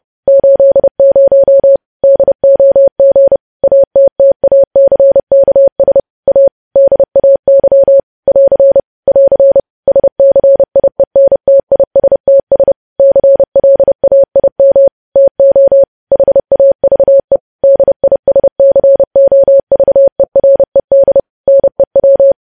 Morse code, heard as a keyed tone, about 30 wpm, 550 Hz. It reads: 80 DOG ATTACKS A DAY <AR> <AR> SCIENTISTS CLAIM TO HAVE DISCOVERED NEW
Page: news headlines in morse code at 30 wpm